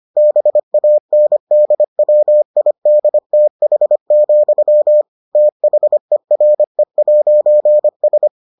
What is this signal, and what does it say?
Morse code, heard as a keyed tone, about 25 wpm, 600 Hz. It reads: BANDWIDTH, THERE'S